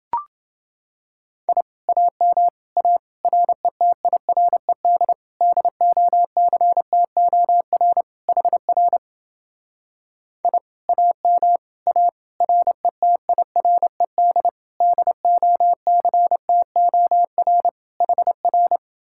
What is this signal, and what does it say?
Morse code, heard as a keyed tone, about 30 wpm, 700 Hz. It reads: I AM A RETIRED DOCTOR HR I AM A RETIRED DOCTOR HR